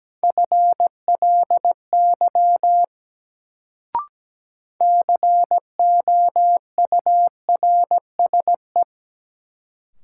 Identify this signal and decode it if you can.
Morse code, heard as a keyed tone, about 17 wpm, 700 Hz. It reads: FLY COURSE
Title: Sentences from Top 500 Words - New Word Drill - 17wpm